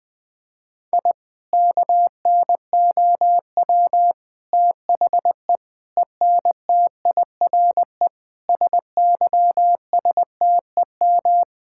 Morse code, heard as a keyed tone, about 20 wpm, 700 Hz. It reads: I KNOW THE ENTIRE SYSTEM
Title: Sentences from Top 1000 Words - Review Part 5 - 20wpm